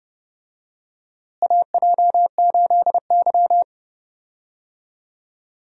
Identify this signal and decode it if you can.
Morse code, heard as a keyed tone, about 30 wpm, 700 Hz. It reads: AJ8Y
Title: USA Call Signs - 4 Characters - Repeated 3x - 30wpm